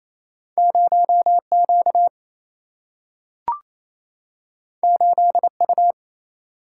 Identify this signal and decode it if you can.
Morse code, heard as a keyed tone, about 28 wpm, 700 Hz. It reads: 0Q 8U
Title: Single-Letter Single-Number 28wpm